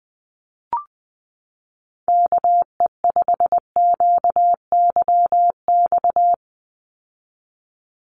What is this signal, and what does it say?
Morse code, heard as a keyed tone, about 20 wpm, 700 Hz. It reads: KE5QYX
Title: USA Call Signs 20wpm